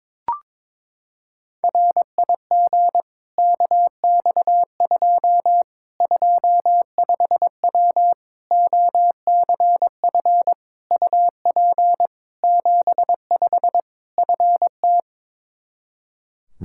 Morse code, heard as a keyed tone, about 22 wpm, 700 Hz. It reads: RIG KX2 25W OCF UP 75 FT
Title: K1USN SST QSO Practice 22wpm